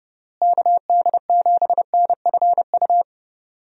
Morse code, heard as a keyed tone, about 30 wpm, 700 Hz. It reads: KD7NFU
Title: USA Call Signs - 6 Characters - Repeated 2x - 30wpm